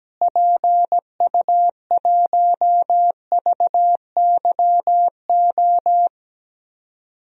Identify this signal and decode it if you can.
Morse code, heard as a keyed tone, about 17 wpm, 700 Hz. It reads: PU1VYO
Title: DXCC-100 Call Signs - 6 Characters - Repeated 2x - 17wpm